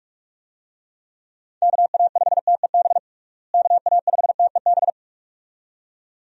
Morse code, heard as a keyed tone, about 46 wpm, 700 Hz. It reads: KA5TEB KA5TEB
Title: USA Call Signs - 6 Characters - Repeated 2x - 45wpm